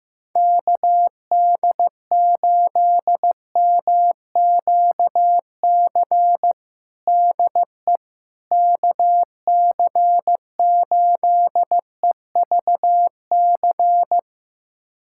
Morse code, heard as a keyed tone, about 16 wpm, 700 Hz. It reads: KD8MQC DE KC8EVC
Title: USA Callsign DE Callsign 15wpm